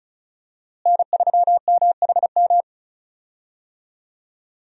Morse code, heard as a keyed tone, about 36 wpm, 700 Hz. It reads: N3MHM
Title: USA Call Signs - 5 Characters - Repeated 3x - 35wpm